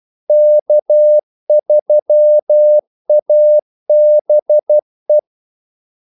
Morse code, heard as a keyed tone, about 12 wpm, 600 Hz. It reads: K3ABE
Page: 2 ekstra ordmellemrum mellem kaldesignalerne